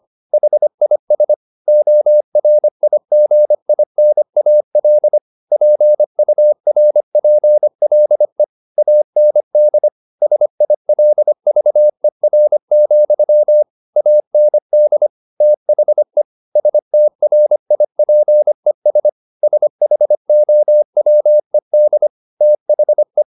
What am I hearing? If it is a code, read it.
Morse code, heard as a keyed tone, about 25 wpm, 600 Hz. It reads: HIS ORIGINAL PURPLE AND SILVER, AND THE STRIPES SHOWED THE